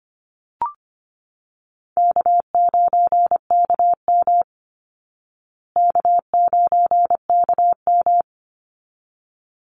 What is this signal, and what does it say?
Morse code, heard as a keyed tone, about 25 wpm, 700 Hz. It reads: K9KM K9KM
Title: Easy Call Signs Repeated 2x - 25wpm